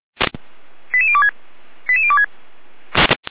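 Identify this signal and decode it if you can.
Five-tone selective call (ZVEI), heard as quick runs of five tones, tones of about 70 ms. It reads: 88026 88026
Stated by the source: ZVEI five-tone call received by SDR.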